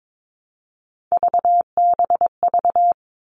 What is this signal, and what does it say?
Morse code, heard as a keyed tone, about 22 wpm, 700 Hz. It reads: VBV
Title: All Cut Numbers - 3-Digit 22wpm